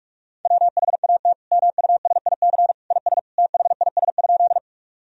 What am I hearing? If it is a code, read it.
Morse code, heard as a keyed tone, about 46 wpm, 700 Hz. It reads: WHAT MUSIC IS THIS?
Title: Sentences from Top 1000 Words - Review Part 1 - 45wpm